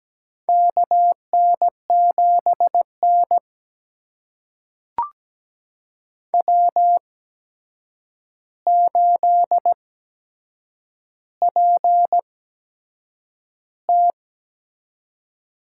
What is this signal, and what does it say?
Morse code, heard as a keyed tone, about 17 wpm, spaced as for 7 wpm, 700 Hz. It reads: KN7N W8PT
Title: USA Call Signs - 4 Characters - 8x Character Spacing - 17wpm